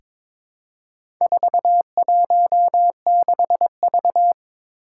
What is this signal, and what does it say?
Morse code, heard as a keyed tone, about 22 wpm, 700 Hz. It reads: 416V
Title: Four Challenging Letter-Number Combinations 22wpm